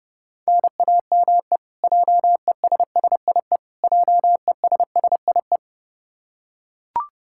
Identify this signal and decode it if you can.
Morse code, heard as a keyed tone, about 32 wpm, 700 Hz. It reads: NAME JESSIE JESSIE
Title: Practice QSO - Exchange 1 - Beginning - 30wpm